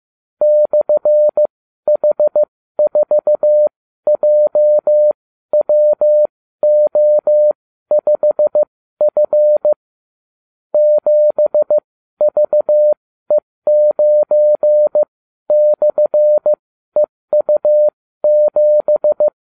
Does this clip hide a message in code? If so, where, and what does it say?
Yes: Morse code from 0.4 to 19.3 s, heard as a keyed tone, about 15 wpm, spaced as for 12 wpm, 600 Hz. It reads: /H4JWO5F 7VE9/EU7